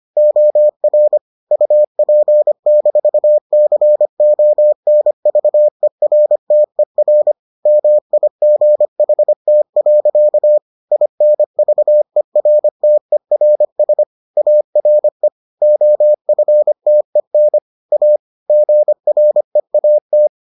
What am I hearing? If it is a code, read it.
Morse code, heard as a keyed tone, about 25 wpm, 600 Hz. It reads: OR UP-CONVERTER MIGHT. INVERTERS ARE OFTEN A GREAT